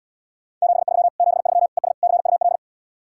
Morse code, heard as a keyed tone, about 77 wpm, 700 Hz. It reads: 73 73 ES CUL